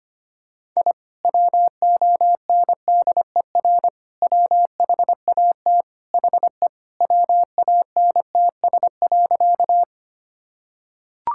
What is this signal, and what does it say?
Morse code, heard as a keyed tone, about 25 wpm, 700 Hz. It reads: I WONDER WHAT HE WANTS.
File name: practice / 25wpm / Sentences with 2-Word Phrases - 25wpm